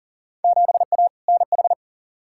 Morse code, heard as a keyed tone, about 40 wpm, 700 Hz. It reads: 7A NH
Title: Field Day Exchanges - 40wpm